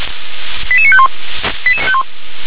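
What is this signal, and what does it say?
Five-tone selective call (ZVEI), heard as quick runs of five tones, tones of about 80 ms. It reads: 88951
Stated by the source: Five-tone ZVEI selective call received on an SDR, address 88771, channel 357.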